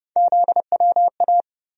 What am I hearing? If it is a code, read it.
Morse code, heard as a keyed tone, about 31 wpm, 700 Hz. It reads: ZWA
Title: USA Call Sign Postfixes - 3 Letters 30wpm